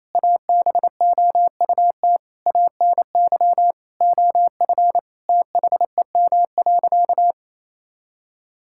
Morse code, heard as a keyed tone, about 28 wpm, 700 Hz. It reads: ABOUT ANY OF THEM.
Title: Sentences from Top 1000 Words - Review Part 6 - 28wpm